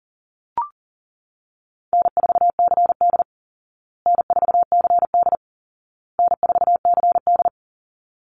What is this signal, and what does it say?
Morse code, heard as a keyed tone, about 40 wpm, 700 Hz. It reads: N4CD N4CD N4CD
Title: Easy Call Signs Repeated 3x - 40wpm